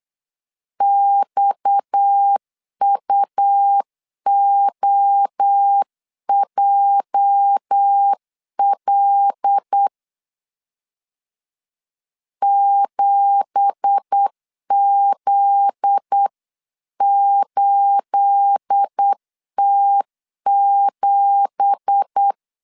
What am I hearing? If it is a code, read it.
Morse code, heard as a keyed tone, about 8 wpm, 800 Hz. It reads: XUOJL 7Z8T7
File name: CW / DET / 1 9 1500C S A 8wpm 5637